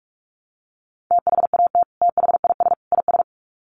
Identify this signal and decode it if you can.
Morse code, heard as a keyed tone, about 44 wpm, 700 Hz. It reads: THAT THIS IS
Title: Sets of 3 Words 45wpm